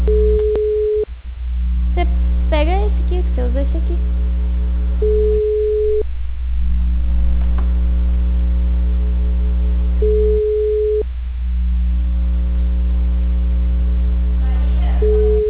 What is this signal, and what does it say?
Ringback tone (Europe, 425 Hz), with speech.